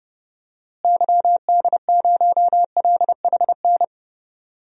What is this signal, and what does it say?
Morse code, heard as a keyed tone, about 31 wpm, 700 Hz. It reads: YD0LHN
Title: DXCC-100 Call Signs - 6 Characters - Repeated 2x - 30wpm